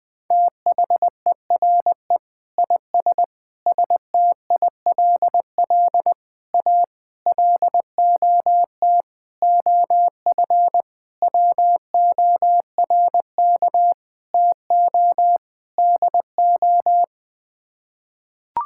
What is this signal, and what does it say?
Morse code, heard as a keyed tone, about 20 wpm, 700 Hz. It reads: THERE IS STILL A LOT OF WORK TO DO
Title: Sets of 9 Words - 20wpm